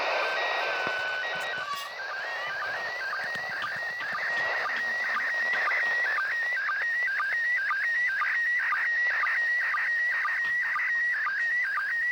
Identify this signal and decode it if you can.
SSTV (slow-scan television), PD 120.
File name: ARISS-SSTV-Expedition-63-ARISS-Series-16-Satellites